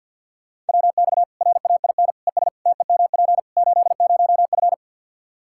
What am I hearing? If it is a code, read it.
Morse code, heard as a keyed tone, about 46 wpm, 700 Hz. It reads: WX RAIN ES TEMP 80F
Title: K1USN SST QSO Practice 50wpm